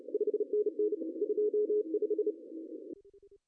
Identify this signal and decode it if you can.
Morse code, heard as a keyed tone, about 31 wpm, 410 Hz. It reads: 5NN 25
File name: CW / cqww-cw-k5tr